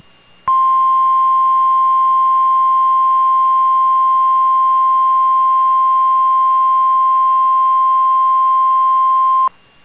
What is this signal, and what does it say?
Emergency Alert System (SAME) tones (the 1050 Hz attention tone). It is heard from 0.4 to 9.5 s.